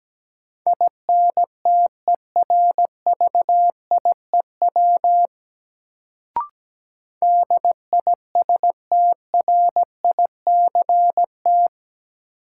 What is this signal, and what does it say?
Morse code, heard as a keyed tone, about 17 wpm, 700 Hz. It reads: INTERVIEW DISTRICT
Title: Sentences from Top 1600 Words - New Word Drill - 17wpm